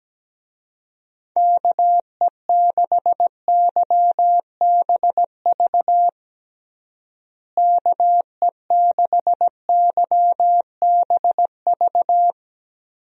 Morse code, heard as a keyed tone, about 17 wpm, 700 Hz. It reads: KE6YBV KE6YBV
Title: USA Call Signs - 6 Characters - Repeated 3x - 17wpm